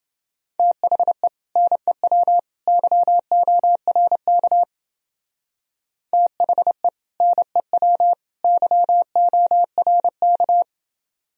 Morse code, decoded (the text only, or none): THE NEW YORK THE NEW YORK